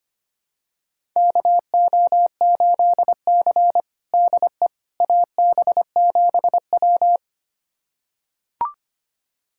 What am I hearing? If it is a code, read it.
Morse code, heard as a keyed tone, about 25 wpm, 700 Hz. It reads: KO8C DE AB7W
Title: USA Callsign DE Callsign - 4 Characters - 25wpm